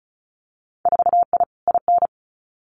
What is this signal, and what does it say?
Morse code, heard as a keyed tone, about 36 wpm, 700 Hz. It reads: 4I IN
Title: Winter Field Day Exchanges Encore 35wpm